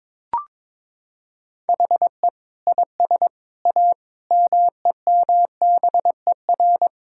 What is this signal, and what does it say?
Morse code, heard as a keyed tone, about 22 wpm, 700 Hz. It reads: HE IS A MEMBER
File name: practice / 22wpm / Sentences with 3-Word Phrases - 22wpm